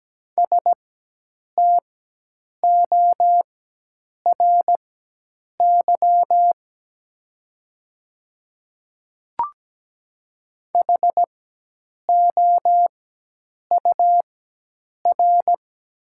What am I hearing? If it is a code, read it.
Morse code, heard as a keyed tone, about 17 wpm, spaced as for 8 wpm, 700 Hz. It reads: STORY HOUR
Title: 4x Character Spacing - Top 500 Words - 17wpm